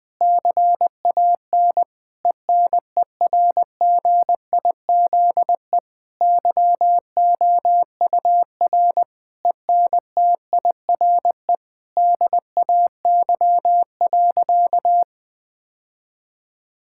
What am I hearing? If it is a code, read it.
Morse code, heard as a keyed tone, about 20 wpm, 700 Hz. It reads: CAN ENERGIZE YOUR ENTIRE DAY.
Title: Difficult Sentences with Hints - 20wpm